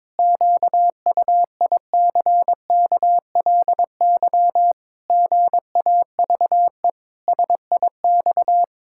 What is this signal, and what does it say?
Morse code, heard as a keyed tone, about 22 wpm, 700 Hz. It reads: QUICKLY GAVE SIX
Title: Pangram Challenge v2 - 22wpm